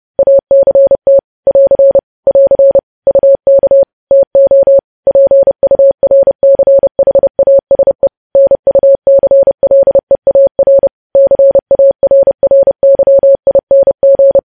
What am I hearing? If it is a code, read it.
Morse code, heard as a keyed tone, about 30 wpm, 550 Hz. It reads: ACT <AR> <AR> UK TO PURCHASE NUCLEAR CARRYING
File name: podcast / News headlines-1x-30x30